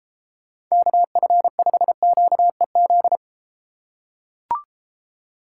Morse code, heard as a keyed tone, about 33 wpm, 700 Hz. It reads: KF5QEZ